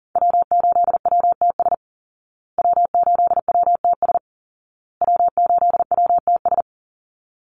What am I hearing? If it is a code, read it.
Morse code, heard as a keyed tone, about 40 wpm, 700 Hz. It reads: W8WTS W8WTS W8WTS
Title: Call Signs Repeated 2x - 40wpm